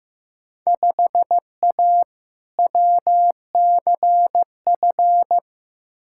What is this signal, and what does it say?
Morse code, heard as a keyed tone, about 15 wpm, 700 Hz. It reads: 5A WCF
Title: Field Day Exchanges Encore - 15wpm